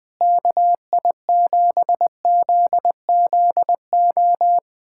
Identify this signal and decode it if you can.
Morse code, heard as a keyed tone, about 20 wpm, 700 Hz. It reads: KI7ZZO
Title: USA Callsign DE Callsign - 6 Characters - 20wpm